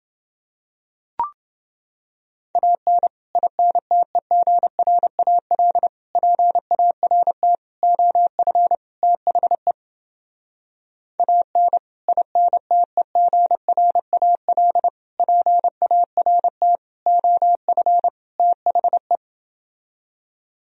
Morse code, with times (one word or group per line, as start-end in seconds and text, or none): 2.6-3.1 AN
3.4-5.9 INTEGRAL
6.2-7.5 PART
7.8-8.7 OF
9.0-9.7 THE
11.2-11.8 AN
12.1-14.9 INTEGRAL
15.2-16.8 PART
17.1-18.1 OF
18.4-19.2 THE